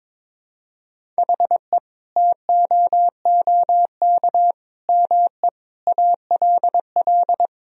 Morse code, heard as a keyed tone, about 22 wpm, 700 Hz. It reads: HE TOOK ME ALL
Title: Sentences from Top 1000 Words - Review Part 1 - 22wpm